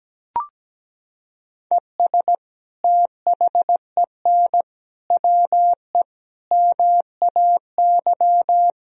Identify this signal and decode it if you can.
Morse code, heard as a keyed tone, about 17 wpm, 700 Hz. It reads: ES THEN WE MAY
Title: Sentences from Top 300 Words and Abbreviations 17wpm